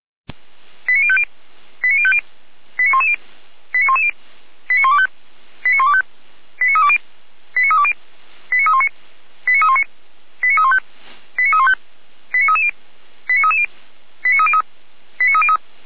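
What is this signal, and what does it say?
Five-tone selective call (ZVEI), heard as quick runs of five tones, tones of about 70 ms. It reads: 89050 89050 89110 89110 89125 89125 89320 89320 89319 89319 89315 89315 89330 89330 89303 89303
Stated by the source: Five-tone ZVEI selective call 89140 received by SDR, feuerwehr-orlamuende.org fire brigade audio server.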